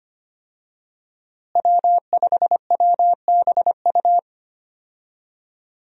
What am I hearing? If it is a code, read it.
Morse code, heard as a keyed tone, about 25 wpm, 700 Hz. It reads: W5WBU
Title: USA Call Signs - 5 Characters - Repeated 2x - 25wpm